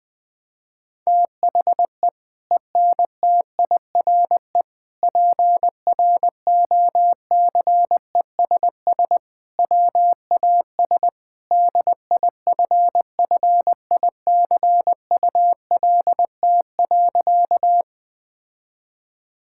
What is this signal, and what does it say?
Morse code, heard as a keyed tone, about 20 wpm, 700 Hz. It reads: THE ENTIRE PROCESS WAS DIFFICULT.
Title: Sentences from Top 1000 Words - Review Part 5 - 20wpm